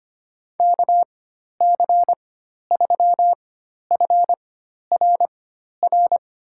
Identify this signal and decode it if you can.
Morse code, heard as a keyed tone, about 25 wpm, spaced as for 12 wpm, 700 Hz. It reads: KC3FRR
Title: USA Call Signs - 6 Characters - 4x Character Spacing - 25wpm